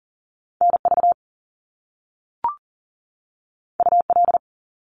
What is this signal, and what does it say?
Morse code, heard as a keyed tone, about 40 wpm, 700 Hz. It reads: NV UL
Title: Top Two Letter Combinations in Words 40wpm